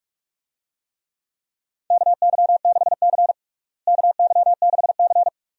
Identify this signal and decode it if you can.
Morse code, heard as a keyed tone, about 46 wpm, 700 Hz. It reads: KY6C KY6C
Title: USA Call Signs - 4 Characters - Repeated 2x - 45wpm